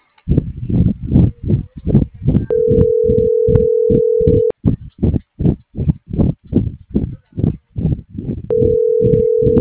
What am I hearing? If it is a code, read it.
Ringback tone (North America), 440 + 480 Hz.